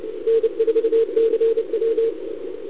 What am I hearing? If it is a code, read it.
Morse code, heard as a keyed tone, about 30 wpm, 440 Hz. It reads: N4CW
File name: CQWWCW2021 / VA2UR